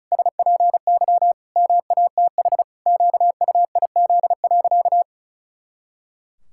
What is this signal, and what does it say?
Morse code, heard as a keyed tone, about 36 wpm, 700 Hz. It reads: SPY MATH QUIZ.
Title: Pangram Challenge v2 - 35wpm